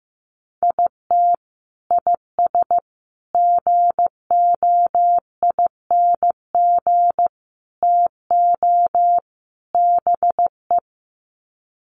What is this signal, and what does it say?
Morse code, heard as a keyed tone, about 15 wpm, 700 Hz. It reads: IT IS GOING TO BE
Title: Sets of 5 Words 15wpm